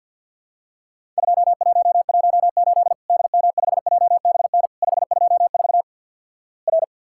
Morse code, heard as a keyed tone, about 50 wpm, 700 Hz. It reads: 2118 DM5JBN 524 R
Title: Worked All Europe DX Contest - QTC Exchange with Repeats - 50wpm